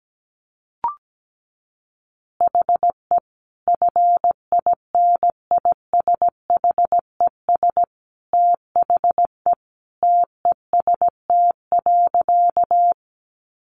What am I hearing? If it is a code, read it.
Morse code, heard as a keyed tone, about 17 wpm, 700 Hz. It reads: HE FINISHES THE TEST.
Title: Additional Easy Phrases 17wpm